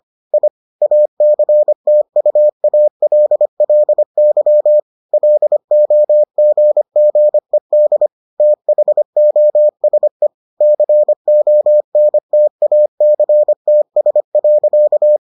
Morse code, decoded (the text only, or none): I ACTUALLY LOGGED THOSE CONTACTS.